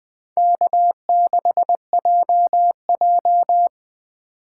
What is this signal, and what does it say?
Morse code, heard as a keyed tone, about 24 wpm, 700 Hz. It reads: K6JJ
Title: Speed Racing - USA Call Signs 20wpm